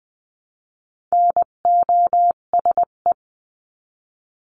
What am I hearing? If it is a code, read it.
Morse code, heard as a keyed tone, about 20 wpm, 700 Hz. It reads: NOSE
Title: 258 Lesson 42c - Words 16-20wpm